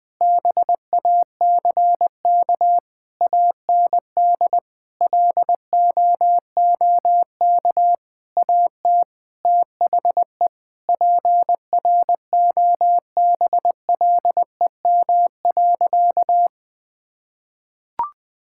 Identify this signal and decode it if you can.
Morse code, heard as a keyed tone, about 20 wpm, 700 Hz. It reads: BACK AND LOOK AT THE PROBLEM.
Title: Sentences with 7-Word Phrases - 20wpm